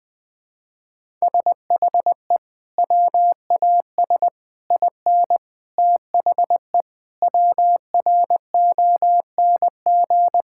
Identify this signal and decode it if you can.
Morse code, heard as a keyed tone, about 20 wpm, 700 Hz. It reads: SHE WAS IN THE WRONG
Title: Sentences with 9-Word Phrases - 20wpm